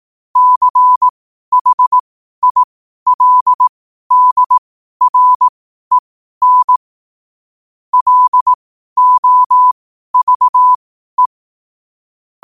Morse code, heard as a keyed tone, about 18 wpm, spaced as for 12 wpm, 1000 Hz. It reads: CHILDREN LOVE
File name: qotdfiles / 12wpm